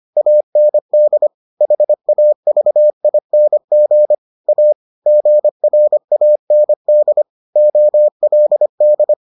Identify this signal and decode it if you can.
Morse code, heard as a keyed tone, about 25 wpm, 600 Hz. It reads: AND HAVING A GRAND OLD